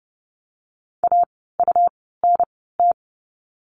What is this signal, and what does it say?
Morse code, heard as a keyed tone, about 30 wpm, spaced as for 17 wpm, 700 Hz. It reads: AUNT
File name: Lesson 17f - Letter u - Words 30wpm